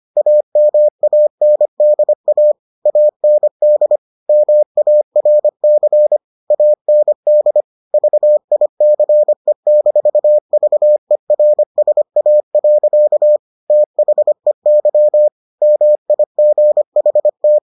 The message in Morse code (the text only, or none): AMANDA AND MARC AND VICE-VERSA. THEY MIGHT